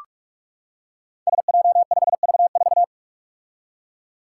Morse code, heard as a keyed tone, about 41 wpm, 700 Hz. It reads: SJ5V4
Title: Five Challenging Letter-Number Combinations 45wpm